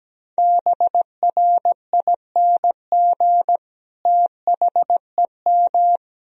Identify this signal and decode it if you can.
Morse code, heard as a keyed tone, about 17 wpm, 700 Hz. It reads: BRING THEM
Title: Sentences from Top 500 Words and Abbreviations 17wpm